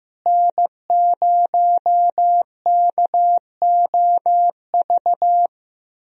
Morse code, heard as a keyed tone, about 15 wpm, 700 Hz. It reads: N0KOV
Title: USA Call Signs - 5 Characters - Repeated 3x - 15wpm